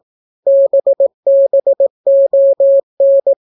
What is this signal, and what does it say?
Morse code, heard as a keyed tone, about 18 wpm, 550 Hz. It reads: BBON